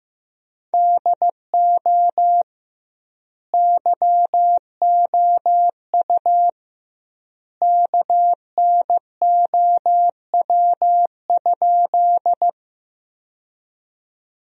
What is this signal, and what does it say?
Morse code, heard as a keyed tone, about 16 wpm, 700 Hz. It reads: DO YOU KNOW?
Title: 2x Word-Spacing - Sentences from Top 100 Words - 15wpm